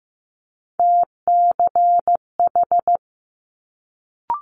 Morse code, heard as a keyed tone, about 15 wpm, 700 Hz. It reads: TCH
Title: Multi-letter Phonemes Quick Fire 15wpm